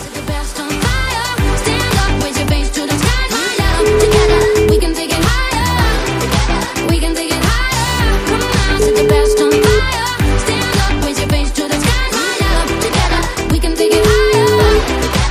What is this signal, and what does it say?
Ringback tone (Europe, 425 Hz), with noise.